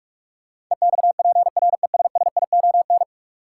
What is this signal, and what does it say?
Morse code, heard as a keyed tone, about 41 wpm, 700 Hz. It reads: EXPRESSION
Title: Sentences from Top 1200 Words - New Word Drill Alt - 45wpm